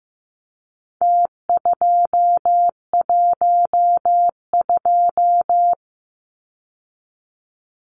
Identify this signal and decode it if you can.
Morse code, heard as a keyed tone, about 15 wpm, 700 Hz. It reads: T212
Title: Numbers - Contest Style - 4-Digit 15wpm